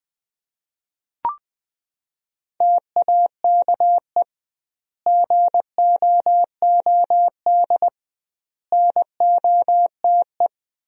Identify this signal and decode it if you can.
Morse code, heard as a keyed tone, about 21 wpm, 700 Hz. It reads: TAKE GOOD NOTE
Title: 2x Word-Spacing - Sentences from Top 500 Words - 20wpm